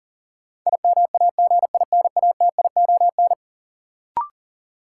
Morse code, heard as a keyed tone, about 40 wpm, 700 Hz. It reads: IMAGINATION
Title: Sentences from Top 1300 Words - New Word Drill Alt - 40wpm